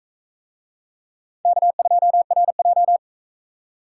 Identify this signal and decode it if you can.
Morse code, heard as a keyed tone, about 41 wpm, 700 Hz. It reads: K2RJ